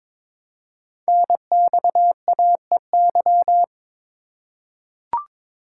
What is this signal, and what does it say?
Morse code, heard as a keyed tone, about 22 wpm, 700 Hz. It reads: NXAEY
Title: Code Groups - 5 Letters - 22wpm